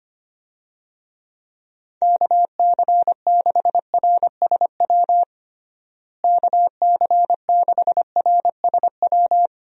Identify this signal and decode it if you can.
Morse code, heard as a keyed tone, about 25 wpm, 700 Hz. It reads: KC6RSW KC6RSW
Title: USA Call Signs - 6 Characters - Repeated 3x - 25wpm